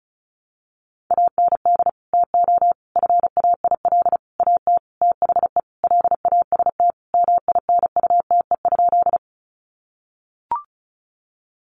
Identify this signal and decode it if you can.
Morse code, heard as a keyed tone, about 34 wpm, 700 Hz. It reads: AND TO FAIL AT THE LAST MINUTE?
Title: Sentences from Top 1100 Words 35wpm